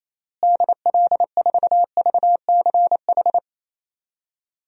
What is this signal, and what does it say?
Morse code, heard as a keyed tone, about 28 wpm, 700 Hz. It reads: DL4VCH
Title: DXCC-100 Call Signs - 6 Characters - Repeated 2x - 28wpm